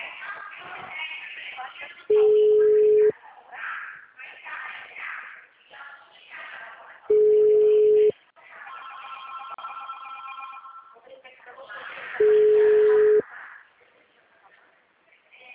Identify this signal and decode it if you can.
Ringback tone (Europe, 425 Hz), with speech.
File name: out- -2045- - - .7094